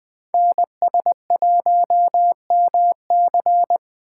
Morse code, heard as a keyed tone, about 21 wpm, 700 Hz. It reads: NS1MC
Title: USA Call Signs - 8x Character Spacing - 20wpm